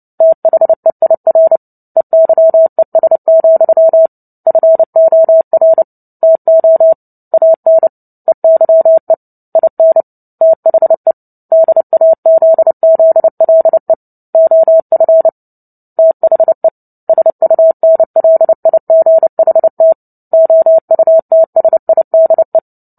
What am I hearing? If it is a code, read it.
Morse code, heard as a keyed tone, about 30 wpm, 650 Hz. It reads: THEIR EYES, FOR TO AN EYE IN THE DAZZLE OF THE SUNLIGHT OUTSIDE
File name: Book Two / 03-Chapter 3 - 29 WPM